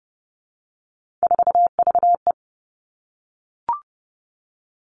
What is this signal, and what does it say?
Morse code, heard as a keyed tone, about 30 wpm, 700 Hz. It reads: 4VE
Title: Three Challenging Letter-Number Combinations 30wpm